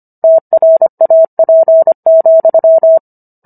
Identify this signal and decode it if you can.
Morse code, heard as a keyed tone, about 25 wpm, 650 Hz. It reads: TRAP,